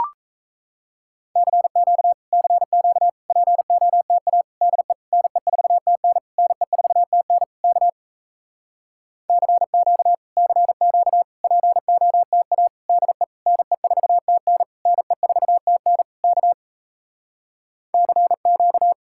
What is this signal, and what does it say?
Morse code, heard as a keyed tone, about 38 wpm, 700 Hz. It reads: CQ CQ POTA DE NE4TN NE4TN K CQ CQ POTA DE NE4TN NE4TN K CQ